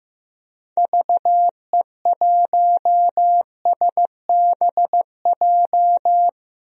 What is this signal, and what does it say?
Morse code, heard as a keyed tone, about 15 wpm, 700 Hz. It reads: VE1SBJ
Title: DXCC-100 Call Signs - 6 Characters - 15wpm